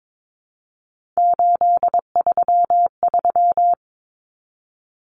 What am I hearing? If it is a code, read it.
Morse code, heard as a keyed tone, about 22 wpm, 700 Hz. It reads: 833
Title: Common Cut Numbers - 3-Digit 22wpm